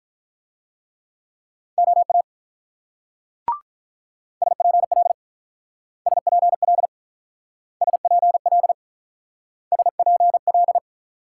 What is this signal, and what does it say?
Morse code, heard as a keyed tone, about 48 wpm, 700 Hz. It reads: OA SPL SPL SPL SPL